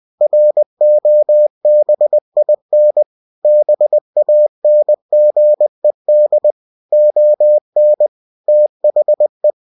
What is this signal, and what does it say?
Morse code, heard as a keyed tone, about 20 wpm, 600 Hz. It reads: ROBIN BANGED ON THE
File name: cw-books / winnie-the-pooh / winnie-20wpm-page-0107